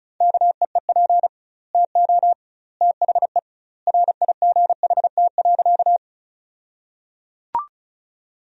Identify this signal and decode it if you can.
Morse code, heard as a keyed tone, about 34 wpm, 700 Hz. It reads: KEEP TO THE RIGHT.
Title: 2x Word-Spacing - Sentences from Top 300 Words - 35wpm